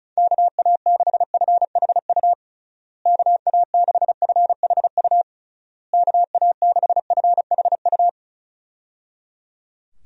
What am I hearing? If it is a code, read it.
Morse code, heard as a keyed tone, about 34 wpm, 700 Hz. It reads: KA6FHU KA6FHU KA6FHU
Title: USA Call Signs - 6 Characters - Repeated 3x - 35wpm